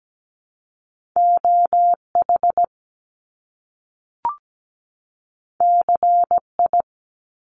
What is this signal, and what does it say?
Morse code, heard as a keyed tone, about 17 wpm, 700 Hz. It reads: OH CI
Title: Top Two Letter Combinations in Words 17wpm